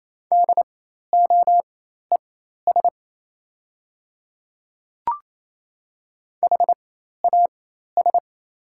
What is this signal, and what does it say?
Morse code, heard as a keyed tone, about 28 wpm, spaced as for 13 wpm, 700 Hz. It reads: DOES HAS
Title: 4x Character Spacing - Top 200 Words - 28wpm